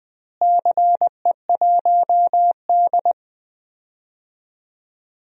Morse code, heard as a keyed tone, about 20 wpm, 700 Hz. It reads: CE1D
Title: DXCC-100 Call Signs - 4 Characters - Repeated 2x - 20wpm